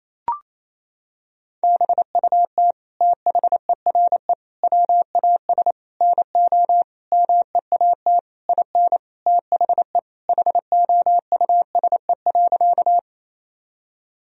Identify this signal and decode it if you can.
Morse code, heard as a keyed tone, about 28 wpm, 700 Hz. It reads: BUT THERE WAS NO MEAT IN THE HOUSE.
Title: Sentences from Top 1000 Words - Review Part 3 - 28wpm